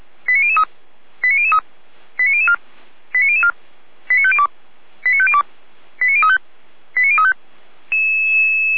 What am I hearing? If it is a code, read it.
Five-tone selective call (ZVEI), heard as quick runs of five tones, tones of about 80 ms. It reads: 89003 89003 89004 89004 89502 89502 89035 89035